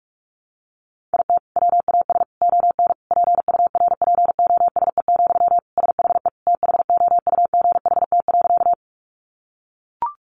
Morse code, heard as a keyed tone, about 44 wpm, 700 Hz. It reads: IT WAS ON PURPOSE, SHE THOUGHT.
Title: Sentences from Top 800 Words 45wpm